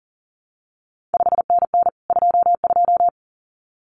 Morse code, heard as a keyed tone, about 39 wpm, 700 Hz. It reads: 5NN 22
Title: CQ WW DX Contest Exchange Encore - 40wpm